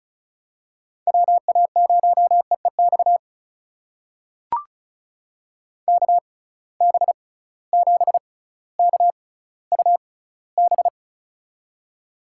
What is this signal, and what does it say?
Morse code, heard as a keyed tone, about 36 wpm, spaced as for 18 wpm, 700 Hz. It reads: WA0EEX KB7KUB
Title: USA Call Signs - 6 Characters - 6x Character Spacing - 35wpm